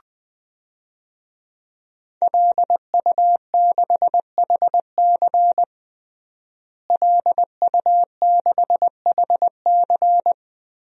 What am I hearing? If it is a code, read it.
Morse code, heard as a keyed tone, about 20 wpm, 700 Hz. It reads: LU6HC LU6HC
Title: DXCC-100 Call Signs - 5 Characters - Repeated 2x - 20wpm